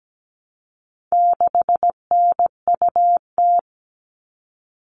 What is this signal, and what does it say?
Morse code, heard as a keyed tone, about 17 wpm, 700 Hz. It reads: 6NUT
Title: All Cut Numbers - 4-Digit 17wpm